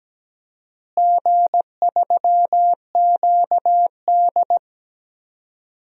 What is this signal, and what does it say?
Morse code, heard as a keyed tone, about 17 wpm, 700 Hz. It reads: G3QD
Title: DXCC-100 Call Signs - 4 Characters - Repeated 3x - 17wpm